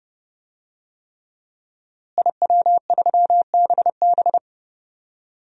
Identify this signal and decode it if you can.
Morse code, heard as a keyed tone, about 32 wpm, 700 Hz. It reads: IW3BB
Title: DXCC-100 Call Signs - 5 Characters - Repeated 3x - 30wpm